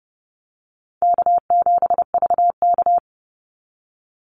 Morse code, heard as a keyed tone, about 30 wpm, 700 Hz. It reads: K7VK
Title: NA SOTA Activator Callsigns 30wpm